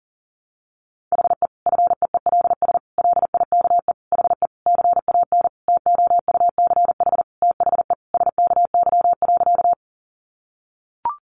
Morse code, heard as a keyed tone, about 40 wpm, 700 Hz. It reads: HE FEELS LIKE HE CAN TOUCH THE SKY.
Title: Additional Easy Phrases 40wpm